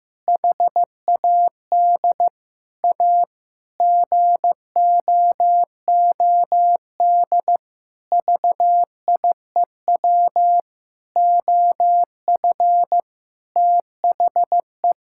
Morse code, heard as a keyed tone, about 15 wpm, 700 Hz. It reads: HAD A GOOD VIEW OF THE
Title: Sentences from Top 1000 Words - Review Part 6 - 15wpm